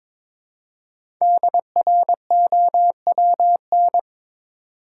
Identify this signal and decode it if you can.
Morse code, heard as a keyed tone, about 22 wpm, 700 Hz. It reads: DROWN
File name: 5 Letter Words - First 3 Letters Given - 22wpm